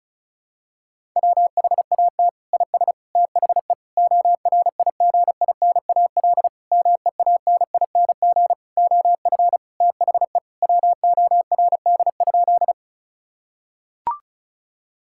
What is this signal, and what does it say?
Morse code, heard as a keyed tone, about 36 wpm, 700 Hz. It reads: WHAT IS THE ORIGINAL MEANING OF THE WORD?
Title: Sentences from Top 1000 Words - Review Part 6 - 35wpm